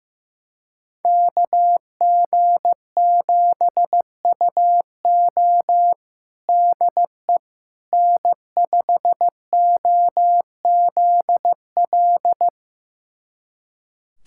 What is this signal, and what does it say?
Morse code, heard as a keyed tone, about 16 wpm, 700 Hz. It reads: KG7UO DE N5OZL
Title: USA Callsign DE Callsign - 5 Characters - 15wpm